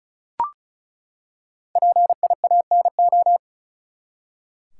Morse code, heard as a keyed tone, about 36 wpm, 700 Hz. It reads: PIANO
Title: Five Letter Words - 35wpm